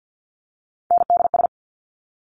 Morse code, heard as a keyed tone, about 50 wpm, 700 Hz. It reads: NDS
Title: Top Three Letter Combinations in Words 50wpm